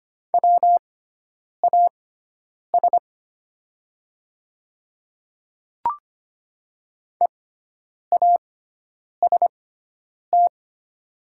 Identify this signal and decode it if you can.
Morse code, heard as a keyed tone, about 25 wpm, spaced as for 9 wpm, 700 Hz. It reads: WAS EAST
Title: 6x Character Spacing - Top 500 Words - 25wpm